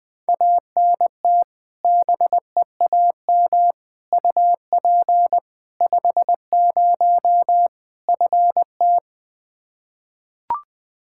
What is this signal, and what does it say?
Morse code, heard as a keyed tone, about 20 wpm, 700 Hz. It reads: ANT BEAM UP 50 FT
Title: Practice QSO - Exchange 2 - Info - 20wpm